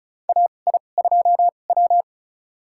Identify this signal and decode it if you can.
Morse code, heard as a keyed tone, about 36 wpm, spaced as for 25 wpm, 700 Hz. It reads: AI2W
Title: USA Call Signs - 4 Characters - 2x Character Spacing - 35wpm